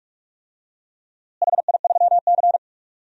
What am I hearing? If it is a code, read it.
Morse code, heard as a keyed tone, about 46 wpm, 700 Hz. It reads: HI3C
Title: DXCC-100 Call Signs - 4 Characters - Repeated 2x - 45wpm